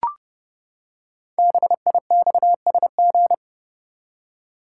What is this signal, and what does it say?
Morse code, heard as a keyed tone, about 30 wpm, 700 Hz. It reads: BIXSG
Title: Code Groups - 5 Letters - 30wpm